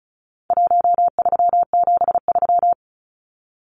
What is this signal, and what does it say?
Morse code, heard as a keyed tone, about 34 wpm, 700 Hz. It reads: 1373
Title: Numbers - Contest Style - 4-Digit 35wpm